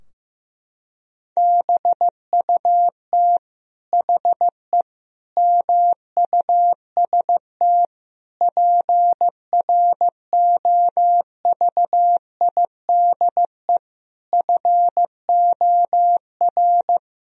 Morse code, heard as a keyed tone, about 15 wpm, 700 Hz. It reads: BUT HE MUST PROVIDE FOR
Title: Sentences from Top 1000 Words - Review Part 3 - 15wpm